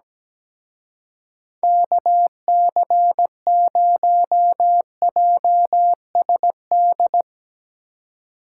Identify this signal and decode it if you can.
Morse code, heard as a keyed tone, about 19 wpm, 700 Hz. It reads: KC0JSD
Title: Speed Racing - USA Call Signs 17wpm